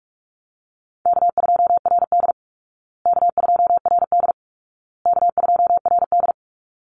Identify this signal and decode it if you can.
Morse code, heard as a keyed tone, about 44 wpm, 700 Hz. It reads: K2RD K2RD K2RD
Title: Easy Call Signs Repeated 3x - 45wpm